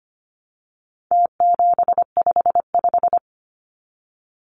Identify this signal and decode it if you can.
Morse code, heard as a keyed tone, about 25 wpm, 700 Hz. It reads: T755
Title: Numbers - Contest Style - 4-Digit 25wpm